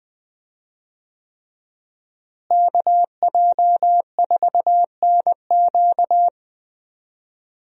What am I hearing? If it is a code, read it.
Morse code, heard as a keyed tone, about 20 wpm, 700 Hz. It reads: KJ4NQ
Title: USA Call Signs - 5 Characters - Repeated 3x - 20wpm